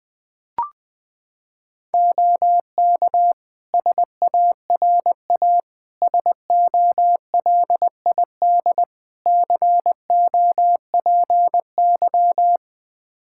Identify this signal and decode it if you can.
Morse code, heard as a keyed tone, about 20 wpm, 700 Hz. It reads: OK SARA SOLID COPY
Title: Practice QSO - Exchange 3 - More Info - 20wpm